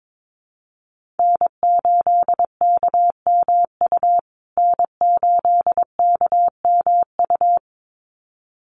Morse code, heard as a keyed tone, about 22 wpm, 700 Hz. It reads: N8KMU N8KMU
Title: USA Call Signs Repeated 2x 22wpm